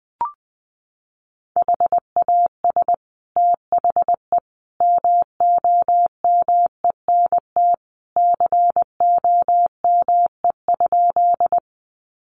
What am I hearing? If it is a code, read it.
Morse code, heard as a keyed tone, about 20 wpm, 700 Hz. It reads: HAS THE MOMENT COME?
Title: Sentences from Top 1000 Words 20wpm